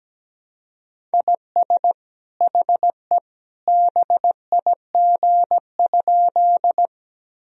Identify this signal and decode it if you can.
Morse code, heard as a keyed tone, about 17 wpm, 700 Hz. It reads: IS HE BIG?
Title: Sentences from Top 1000 Words - Review Part 1 - 17wpm